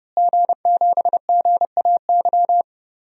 Morse code, heard as a keyed tone, about 30 wpm, 700 Hz. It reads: G7GAY
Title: DXCC-100 Call Signs - 5 Characters - Repeated 3x - 30wpm